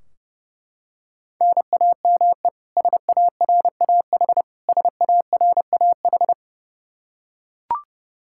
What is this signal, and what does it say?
Morse code, heard as a keyed tone, about 32 wpm, 700 Hz. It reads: NAME SARAH SARAH
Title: Practice QSO - Exchange 1 - Beginning - 30wpm